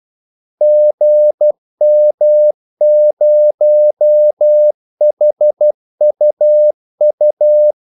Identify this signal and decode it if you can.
Morse code, heard as a keyed tone, about 12 wpm, 600 Hz. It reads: GM0HUU
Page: 2 ekstra ordmellemrum mellem kaldesignalerne